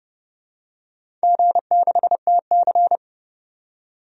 Morse code, heard as a keyed tone, about 30 wpm, 700 Hz. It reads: G6TC
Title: DXCC-100 Call Signs - 4 Characters - 30wpm